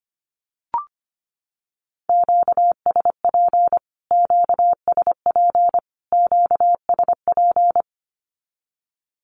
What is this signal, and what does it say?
Morse code, heard as a keyed tone, about 25 wpm, 700 Hz. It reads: QSP QSP QSP
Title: Q-Codes Repeated 3x 25wpm